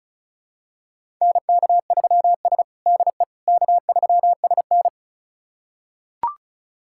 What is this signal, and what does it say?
Morse code, heard as a keyed tone, about 36 wpm, 700 Hz. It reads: NK3S DE K3SN
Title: USA Callsign DE Callsign - 4 Characters - 35wpm